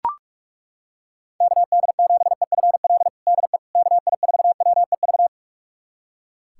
Morse code, heard as a keyed tone, about 42 wpm, 700 Hz. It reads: KD7EFL DE KI4WEV
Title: USA Callsign DE Callsign - 6 Characters - 45wpm